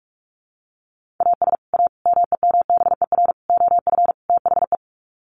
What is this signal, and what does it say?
Morse code, heard as a keyed tone, about 44 wpm, 700 Hz. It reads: AS A MEMBER OF THE
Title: Sets of 5 Words 45wpm